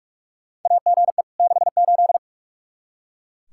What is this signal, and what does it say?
Morse code, heard as a keyed tone, about 44 wpm, 700 Hz. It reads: AGE 68
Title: Practice QSO - Exchange 3 - More Info - 45wpm